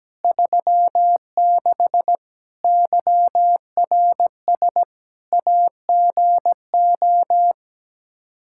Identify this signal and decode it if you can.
Morse code, heard as a keyed tone, about 17 wpm, 700 Hz. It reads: 36 YRS AGO
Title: Practice QSO - Exchange 3 - More Info - 17wpm